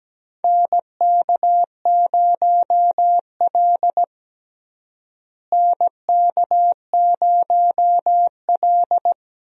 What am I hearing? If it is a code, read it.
Morse code, heard as a keyed tone, about 17 wpm, 700 Hz. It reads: NK0L NK0L
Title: USA Call Signs - 4 Characters - Repeated 3x - 17wpm